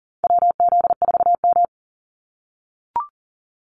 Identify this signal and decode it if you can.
Morse code, heard as a keyed tone, about 38 wpm, 700 Hz. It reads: WZ4M
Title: NA SOTA Activator Callsigns 40wpm